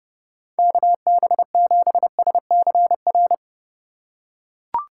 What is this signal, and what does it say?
Morse code, heard as a keyed tone, about 30 wpm, 700 Hz. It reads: KB7SCR
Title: USA Call Signs - 3x Character Spacing - 30wpm